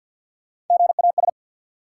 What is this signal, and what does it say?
Morse code, heard as a keyed tone, about 50 wpm, 700 Hz. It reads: GAS
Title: Sentences from Top 1100 Words - New Word Drill - 50wpm